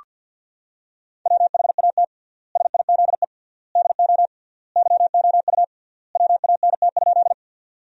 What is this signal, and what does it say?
Morse code, heard as a keyed tone, about 46 wpm, 700 Hz. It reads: WHAT SIZE DO YOU WANT?
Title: 3x Word-Spacing - Sentences from Top 400 Words - 50wpm